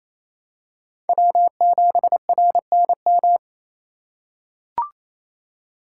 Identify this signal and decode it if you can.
Morse code, heard as a keyed tone, about 28 wpm, 700 Hz. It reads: W7RNM
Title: USA Call Signs - 5 Characters - 6x Character Spacing - 28wpm